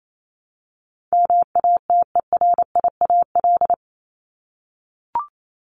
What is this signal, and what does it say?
Morse code, heard as a keyed tone, about 28 wpm, 700 Hz. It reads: MATERIAL
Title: ES:S Suffix - Top 100 Words with Base Words 28wpm